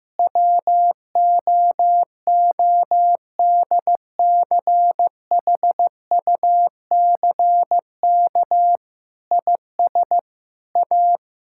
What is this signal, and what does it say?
Morse code, heard as a keyed tone, about 14 wpm, 700 Hz. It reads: WOODCHUCK IS A
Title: Pangram Challenge v2 - 15wpm